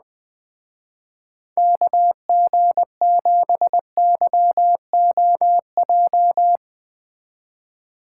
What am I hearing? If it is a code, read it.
Morse code, heard as a keyed tone, about 20 wpm, 700 Hz. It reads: KG7YOJ
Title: Speed Racing - USA 2x3 Call Signs 17wpm